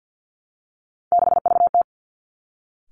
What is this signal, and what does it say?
Morse code, heard as a keyed tone, about 50 wpm, 700 Hz. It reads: 6VT
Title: Three Challenging Letter-Number Combinations 50wpm